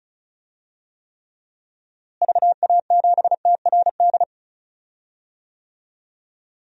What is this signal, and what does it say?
Morse code, heard as a keyed tone, about 36 wpm, 700 Hz. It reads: VA7TRD
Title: DXCC-100 Call Signs - 6 Characters - Repeated 2x - 35wpm